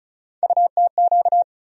Morse code, heard as a keyed tone, about 34 wpm, 700 Hz. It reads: UTQ
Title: USA Call Sign Postfixes - 3 Letters 35wpm